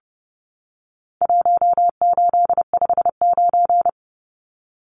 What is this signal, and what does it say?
Morse code, heard as a keyed tone, about 30 wpm, 700 Hz. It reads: 1859
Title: Numbers - 4-Digit 30wpm